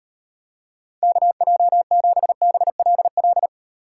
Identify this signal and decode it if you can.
Morse code, heard as a keyed tone, about 38 wpm, 700 Hz. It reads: KJ7BLL